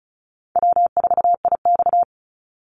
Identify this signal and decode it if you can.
Morse code, heard as a keyed tone, about 34 wpm, 700 Hz. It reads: W4IX
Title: Easy Call Signs Repeated 2x - 35wpm